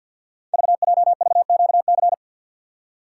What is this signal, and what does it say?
Morse code, heard as a keyed tone, about 50 wpm, 690 Hz. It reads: VJVQC
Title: Code Groups - 5 Letters - 50wpm